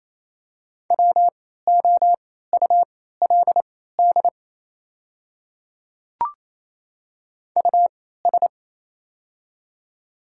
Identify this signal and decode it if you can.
Morse code, heard as a keyed tone, about 27 wpm, spaced as for 16 wpm, 700 Hz. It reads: WOULD US
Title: 3x Character Spacing - Top 300 Words - 28wpm